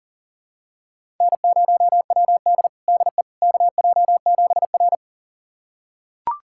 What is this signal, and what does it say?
Morse code, heard as a keyed tone, about 40 wpm, 700 Hz. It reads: N0WD DE KJ7R
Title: USA Callsign DE Callsign - 4 Characters - 40wpm